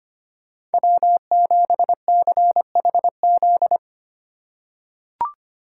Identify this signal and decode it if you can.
Morse code, heard as a keyed tone, about 25 wpm, 700 Hz. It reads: W7CHZ
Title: USA Call Signs - 5 Characters - 2x Character Spacing - 25wpm